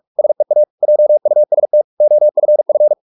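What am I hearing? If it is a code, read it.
Morse code, heard as a keyed tone, about 44 wpm, 600 Hz. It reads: SEA JUST OFF
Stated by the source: Morse practice at 45 wpm, the novel Never by Ken Follett in English, from CW med Gnister.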